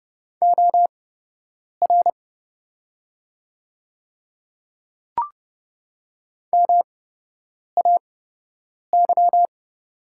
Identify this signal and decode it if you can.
Morse code, heard as a keyed tone, about 31 wpm, spaced as for 8 wpm, 700 Hz. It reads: OR MAY
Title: 8x Character Spacing - Top 100 Words - 30wpm